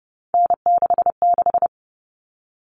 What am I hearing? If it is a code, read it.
Morse code, heard as a keyed tone, about 30 wpm, 700 Hz. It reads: N66
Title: All Cut Numbers - 3-Digit 30wpm